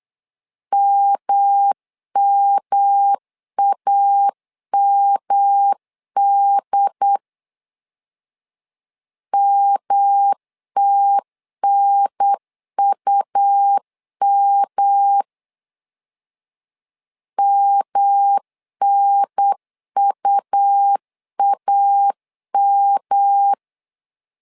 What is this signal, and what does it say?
Morse code, heard as a keyed tone, about 8 wpm, 800 Hz. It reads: MMAMD MTNUM MNUAM
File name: LEZIONI / 3 MWRO / 3 M 8wpm 2300